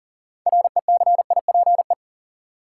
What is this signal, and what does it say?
Morse code, heard as a keyed tone, about 40 wpm, 700 Hz. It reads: RECIPE
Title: Sentences from Top 1200 Words - New Word Drill - 40wpm